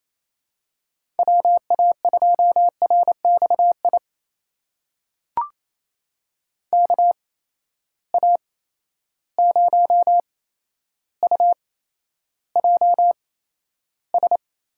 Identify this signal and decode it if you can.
Morse code, heard as a keyed tone, about 28 wpm, spaced as for 12 wpm, 700 Hz. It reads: WA2RXI KA0UJS
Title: USA Call Signs - 6 Characters - 8x Character Spacing - 28wpm